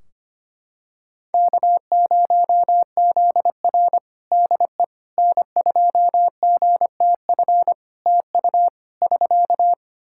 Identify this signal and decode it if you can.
Morse code, heard as a keyed tone, about 25 wpm, 700 Hz. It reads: K0ZR DE N2GTF TU <SK>
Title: Practice QSO - Exchange 4 - Ending - 25wpm